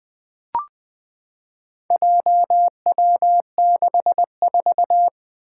Morse code, heard as a keyed tone, about 20 wpm, 700 Hz. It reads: JW64
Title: Four Challenging Letter-Number Combinations 20wpm